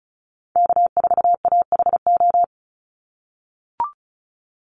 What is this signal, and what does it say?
Morse code, heard as a keyed tone, about 34 wpm, 700 Hz. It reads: K4AHO
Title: Call Signs - 35wpm